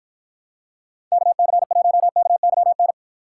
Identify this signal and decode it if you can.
Morse code, heard as a keyed tone, about 53 wpm, 700 Hz. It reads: KC1KYN